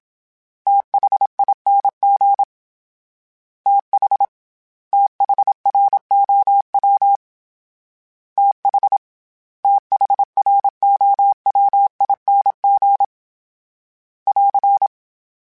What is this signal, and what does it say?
Morse code, heard as a keyed tone, about 27 wpm, 800 Hz. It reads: THING TH THROW TH THROWING <AR>